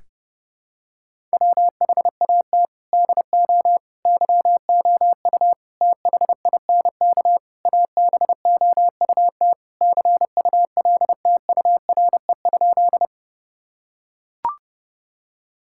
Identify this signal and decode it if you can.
Morse code, heard as a keyed tone, about 32 wpm, 700 Hz. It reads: WHAT DO YOU THINK ABOUT CULTURE?
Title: Sentences from Top 1000 Words - Review Part 6 - 30wpm